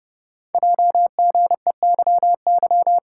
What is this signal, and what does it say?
Morse code, heard as a keyed tone, about 30 wpm, 700 Hz. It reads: JGEYY
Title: Code Groups - 5 Letters - 30wpm